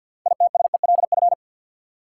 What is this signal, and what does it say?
Morse code, heard as a keyed tone, about 53 wpm, 700 Hz. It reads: ITSELF
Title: Six Letter Words - 50wpm